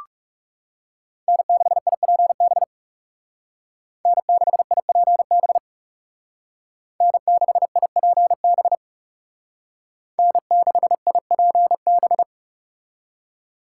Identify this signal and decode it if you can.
Morse code, heard as a keyed tone, about 36 wpm, 700 Hz. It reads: N6IPB N6IPB N6IPB N6IPB
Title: Speed Racing - USA 1x3 Call Signs 30wpm